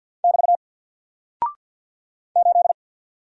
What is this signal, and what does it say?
Morse code, heard as a keyed tone, about 50 wpm, 700 Hz. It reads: <BT> 8
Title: Single Character (Letter:Number:Punctuation:Prosign) 50wpm